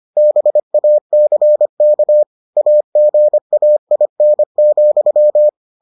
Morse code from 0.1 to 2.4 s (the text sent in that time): BACK